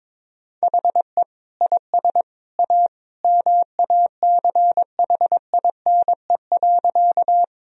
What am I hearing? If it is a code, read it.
Morse code, heard as a keyed tone, about 22 wpm, 700 Hz. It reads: HE IS A MACHINE.
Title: Sentences from Top 1000 Words - Review Part 2 - 22wpm